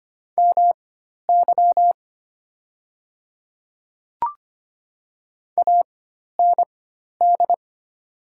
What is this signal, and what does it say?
Morse code, heard as a keyed tone, about 25 wpm, spaced as for 12 wpm, 700 Hz. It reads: MY AND
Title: 4x Character Spacing - Top 100 Words - 25wpm